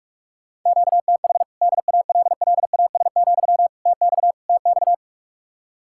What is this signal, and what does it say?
Morse code, heard as a keyed tone, about 44 wpm, 700 Hz. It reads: QTH DALLAS, TX TX